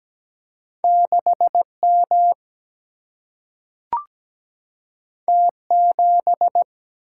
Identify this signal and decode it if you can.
Morse code, heard as a keyed tone, about 17 wpm, 700 Hz. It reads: 6 M T 7
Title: Single-Letter Single-Number 17wpm